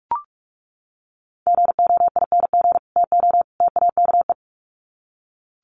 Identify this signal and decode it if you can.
Morse code, heard as a keyed tone, about 44 wpm, 700 Hz. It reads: GOING TO TAKE
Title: Sets of 3 Words 45wpm